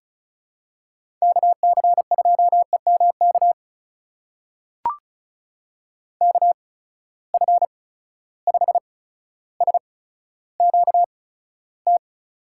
Morse code, heard as a keyed tone, about 36 wpm, spaced as for 15 wpm, 700 Hz. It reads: KC2EMK KF5SQT
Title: USA Call Signs - 6 Characters - 8x Character Spacing - 35wpm